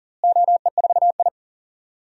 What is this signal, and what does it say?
Morse code, heard as a keyed tone, about 40 wpm, 700 Hz. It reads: OE4I
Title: DXCC-100 Call Signs - 4 Characters - Repeated 3x - 40wpm